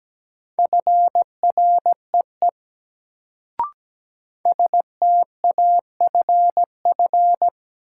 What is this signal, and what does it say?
Morse code, heard as a keyed tone, about 17 wpm, 700 Hz. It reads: FREE STAFF
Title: Sets of 5 Words Encore - New Word Drill - 17wpm